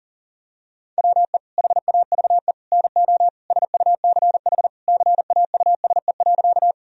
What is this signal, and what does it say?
Morse code, heard as a keyed tone, about 37 wpm, 700 Hz. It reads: WE HAVE NO SUCH CAUSE.
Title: Sentences from Top 1000 Words - Review Part 1 - 40wpm